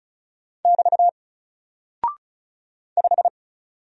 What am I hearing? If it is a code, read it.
Morse code, heard as a keyed tone, about 36 wpm, 700 Hz. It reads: <BT> 5
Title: Single Character (Letter:Number:Punctuation:Prosign) Alt 35wpm